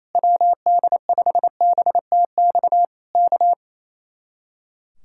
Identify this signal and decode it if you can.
Morse code, heard as a keyed tone, about 27 wpm, 700 Hz. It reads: WD5BTX K
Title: Speed-Racing Common QSO Phrases - 28wpm